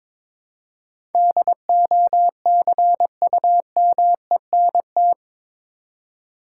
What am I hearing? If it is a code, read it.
Morse code, heard as a keyed tone, about 21 wpm, 700 Hz. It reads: DOCUMENT
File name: practice / 22wpm / Eight Letter Words - 22wpm